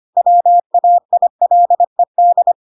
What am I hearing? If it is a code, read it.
Morse code, heard as a keyed tone, about 25 wpm, 700 Hz. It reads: WAILED
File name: Pinocchio / Pinocchio ch5 25 25 ik1hge